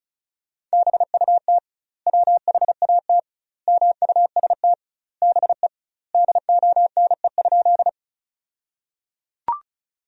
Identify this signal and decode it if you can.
Morse code, heard as a keyed tone, about 36 wpm, 700 Hz. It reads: BUT WHAT MUST BE DONE?
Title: 2x Word-Spacing - Sentences from Top 500 Words - 35wpm